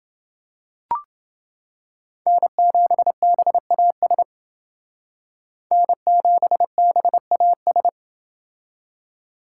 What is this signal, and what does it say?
Morse code, heard as a keyed tone, about 29 wpm, 700 Hz. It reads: N7BAS N7BAS